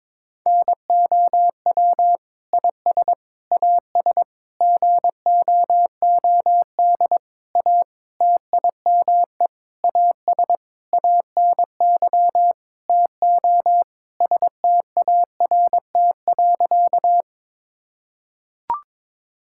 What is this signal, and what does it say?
Morse code, heard as a keyed tone, about 22 wpm, 700 Hz. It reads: NOW IS AS GOOD A TIME AS ANY TO START.
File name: Sentences with 9-Word Phrases - 22wpm